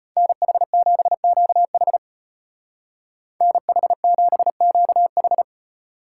Morse code, heard as a keyed tone, about 33 wpm, 700 Hz. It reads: NH7QH NH7QH